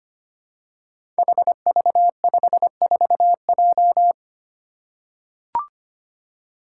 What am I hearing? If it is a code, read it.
Morse code, heard as a keyed tone, about 25 wpm, 700 Hz. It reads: HV54J
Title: Five Challenging Letter-Number Combinations 25wpm